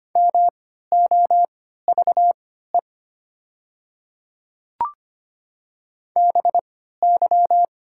Morse code, heard as a keyed tone, about 25 wpm, spaced as for 14 wpm, 700 Hz. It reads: MOVE BY
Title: 3x Character Spacing - Top 300 Words - 25wpm